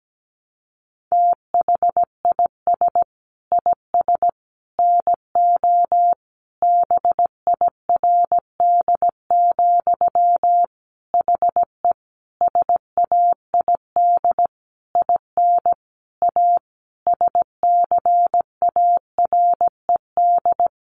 Morse code, heard as a keyed tone, about 17 wpm, 700 Hz. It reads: THIS IS NO BIRD, HE SAID IN A SCARED
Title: Sentences from Top 700 Words 17wpm